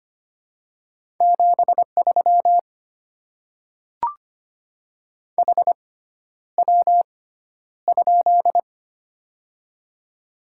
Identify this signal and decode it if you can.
Morse code, heard as a keyed tone, about 25 wpm, spaced as for 11 wpm, 700 Hz. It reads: 73 HW?
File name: Top 20 QSO Elements - 6x Character Spacing - 25wpm